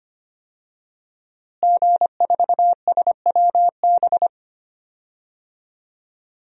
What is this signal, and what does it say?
Morse code, heard as a keyed tone, about 25 wpm, 700 Hz. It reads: G4SWB
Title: DXCC-100 Call Signs - 5 Characters - Repeated 2x - 25wpm